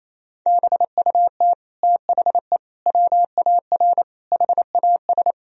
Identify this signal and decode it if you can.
Morse code, heard as a keyed tone, about 27 wpm, 700 Hz. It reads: BUT THE WAR HAS
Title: Sentences from Top 1000 Words - Review Part 2 - 28wpm